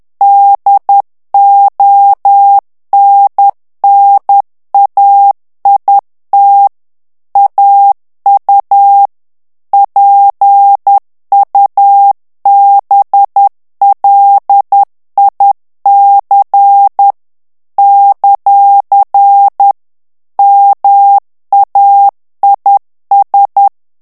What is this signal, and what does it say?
Morse code, heard as a keyed tone, about 10 wpm, 800 Hz. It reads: DONNAIT AU PUBLIC ; MAIS
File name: CW / FICHIERS-MP3 / FRANCAIS TEXTE CLAIR 2 VITESSE 10 MOTS MINUTES